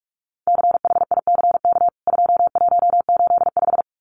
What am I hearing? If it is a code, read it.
Morse code, heard as a keyed tone, about 44 wpm, 700 Hz. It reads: CHICK 2185
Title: CWT Exchange 45wpm